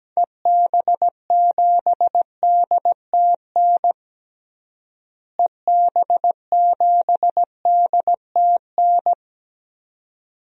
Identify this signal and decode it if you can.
Morse code, heard as a keyed tone, about 17 wpm, 700 Hz. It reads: EB7DTN EB7DTN
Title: DXCC-100 Call Signs - 6 Characters - Repeated 2x - 17wpm